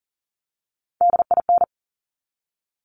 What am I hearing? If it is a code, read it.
Morse code, heard as a keyed tone, about 37 wpm, 700 Hz. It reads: DIN
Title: Top Three Letter Combinations in Words 40wpm